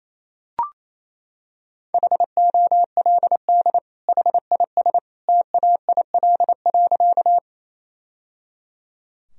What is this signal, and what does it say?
Morse code, heard as a keyed tone, about 28 wpm, 700 Hz. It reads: HOLD HIS TAIL.
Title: Sentences from Top 1000 Words - Review Part 3 - 28wpm